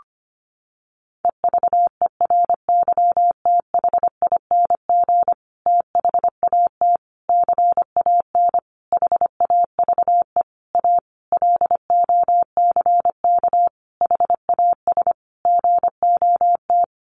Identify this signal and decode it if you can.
Morse code, heard as a keyed tone, about 25 wpm, 700 Hz. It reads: EVERYTHING THAT CAN HAVE A LOCK HAS GOT
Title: Sentences from Top 1300 Words 25wpm